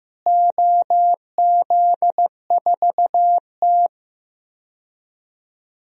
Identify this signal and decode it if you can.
Morse code, heard as a keyed tone, about 15 wpm, 700 Hz. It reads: OZ4T
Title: DXCC-100 Call Signs - 4 Characters - Repeated 2x - 15wpm